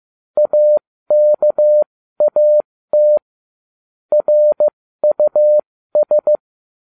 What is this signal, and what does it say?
Morse code, heard as a keyed tone, about 15 wpm, spaced as for 13 wpm, 600 Hz. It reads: AKAT RUS